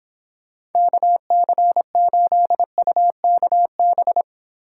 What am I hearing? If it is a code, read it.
Morse code, heard as a keyed tone, about 26 wpm, 700 Hz. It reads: KC8UKB